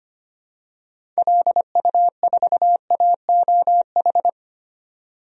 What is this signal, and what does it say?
Morse code, heard as a keyed tone, about 25 wpm, 700 Hz. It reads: LU4AOH
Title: DXCC-100 Call Signs - 6 Characters - 25wpm